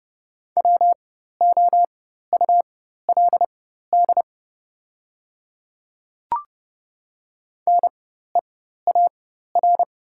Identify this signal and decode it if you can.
Morse code, heard as a keyed tone, about 30 wpm, spaced as for 14 wpm, 700 Hz. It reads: WOULD NEAR
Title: 4x Character Spacing - Top 200 Words - 30wpm